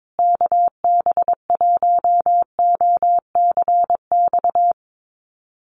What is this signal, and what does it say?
Morse code, heard as a keyed tone, about 22 wpm, 700 Hz. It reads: KB1OCX
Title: USA Call Signs Repeated 3x 22wpm